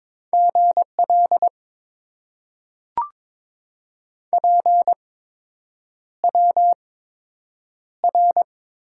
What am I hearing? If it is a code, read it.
Morse code, heard as a keyed tone, about 22 wpm, spaced as for 8 wpm, 700 Hz. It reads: GL PWR
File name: Top 90 QSO Elements - 8x Character Spacing - 22wpm